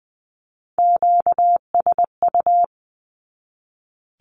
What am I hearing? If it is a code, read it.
Morse code, heard as a keyed tone, about 20 wpm, 700 Hz. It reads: QSU
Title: Q-Codes 20wpm